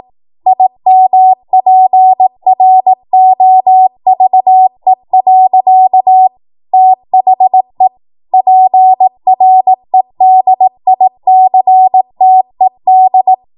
Morse code, heard as a keyed tone, about 18 wpm, 750 Hz. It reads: IMPROVE. THE PREDICTED
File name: Morse / 18WPM